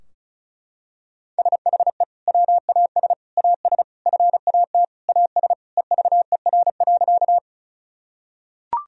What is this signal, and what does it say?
Morse code, heard as a keyed tone, about 36 wpm, 700 Hz. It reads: SHE WAS AS FAT AS EVER.
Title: Sentences from Top 1000 Words - Review Part 4 - 35wpm